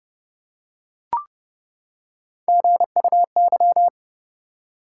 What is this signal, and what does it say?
Morse code, heard as a keyed tone, about 32 wpm, 700 Hz. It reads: GUY
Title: Three Letter Words - 30wpm